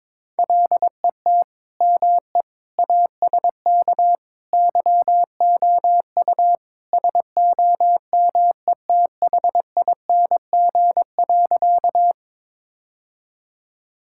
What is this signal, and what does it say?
Morse code, heard as a keyed tone, about 22 wpm, 700 Hz. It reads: LET ME ASK YOU SOMETHING.
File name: practice / 22wpm / Sentences with 4-Word Phrases - 22wpm